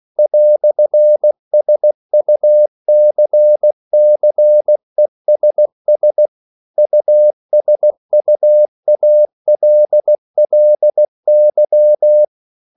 Morse code, heard as a keyed tone, about 16 wpm, 600 Hz. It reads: "SUCCESS USUALLY
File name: CW / Practice / EnglishQuotations0216